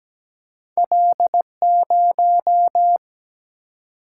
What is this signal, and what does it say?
Morse code, heard as a keyed tone, about 17 wpm, 700 Hz. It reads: L0
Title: Single-Letter Single-Number 17wpm